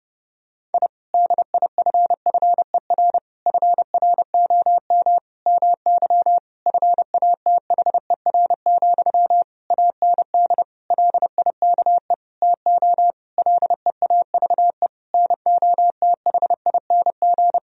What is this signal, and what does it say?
Morse code, heard as a keyed tone, about 30 wpm, 700 Hz. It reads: I DIFFER FROM MY FATHER, AND LIKE TO LEAVE NOTHING
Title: Sentences from Top 1000 Words - Review Part 1 - 30wpm